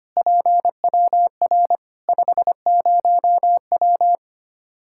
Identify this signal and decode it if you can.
Morse code, heard as a keyed tone, about 25 wpm, 700 Hz. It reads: PWR 50W
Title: Practice QSO - Exchange 2 - Info - 25wpm